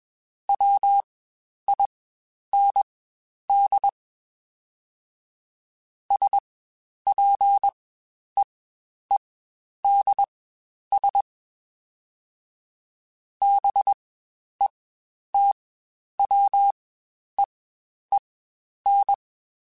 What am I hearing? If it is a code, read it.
Morse code, heard as a keyed tone, about 21 wpm, spaced as for 10 wpm, 800 Hz. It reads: WIND SPEEDS BETWEEN
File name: Spaced Characters / Spaced Characters-mp3s / Weather 4 24 sp @20